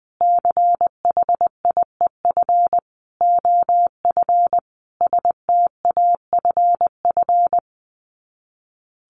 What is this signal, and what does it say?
Morse code, heard as a keyed tone, about 20 wpm, 700 Hz. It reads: CHIEF OF STAFF
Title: Sets of 5 Words Encore 20wpm